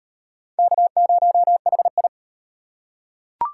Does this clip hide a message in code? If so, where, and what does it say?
Yes: Morse code from 0.6 to 2.1 s, heard as a keyed tone, about 36 wpm, 700 Hz. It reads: K0HI